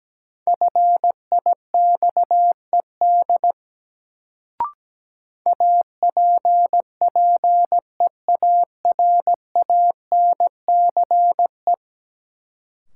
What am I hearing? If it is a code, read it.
Morse code, heard as a keyed tone, about 17 wpm, 700 Hz. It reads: FIXED APPEARANCE
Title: Sentences from Top 1100 Words - New Word Drill Alt - 17wpm